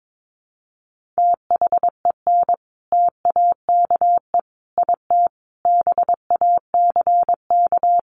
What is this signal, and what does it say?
Morse code, heard as a keyed tone, about 22 wpm, 700 Hz. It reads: THEN TAKE IT BACK
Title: Sentences from Top 1500 Words 22wpm